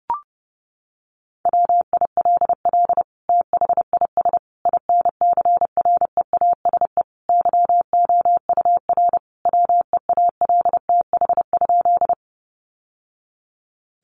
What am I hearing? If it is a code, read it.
Morse code, heard as a keyed tone, about 30 wpm, 700 Hz. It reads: WILL THIS INCREASE YOUR WEALTH?
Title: Sentences from Top 1200 Words 30wpm